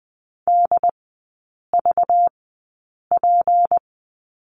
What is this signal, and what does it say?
Morse code, heard as a keyed tone, about 20 wpm, spaced as for 8 wpm, 700 Hz. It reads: DVP
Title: 191 Lesson 31e - Letter v - Three Characters 20wpm